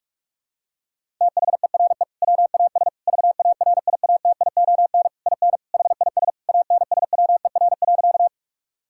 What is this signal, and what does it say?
Morse code, heard as a keyed tone, about 46 wpm, 700 Hz. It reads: THERE WAS VARIATION IN HIS ANSWER.
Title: Sentences from Top 1600 Words 45wpm